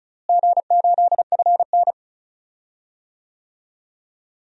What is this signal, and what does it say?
Morse code, heard as a keyed tone, about 36 wpm, 700 Hz. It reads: G8FN
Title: DXCC-100 Call Signs - 4 Characters - Repeated 3x - 35wpm